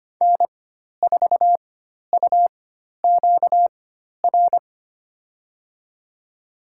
Morse code, heard as a keyed tone, about 25 wpm, spaced as for 12 wpm, 700 Hz. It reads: N4UQR
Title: USA Call Signs - 4x Character Spacing - 25wpm